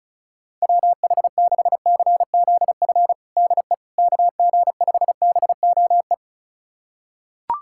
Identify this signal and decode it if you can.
Morse code, heard as a keyed tone, about 36 wpm, 690 Hz. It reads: WH6CZF DE KG5BOE
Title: USA Callsign DE Callsign - 6 Characters - 35wpm